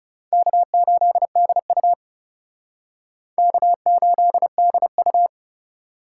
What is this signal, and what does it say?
Morse code, heard as a keyed tone, about 37 wpm, 700 Hz. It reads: K8DU K8DU
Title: Speed Racing - USA 1x2 Call Signs 30wpm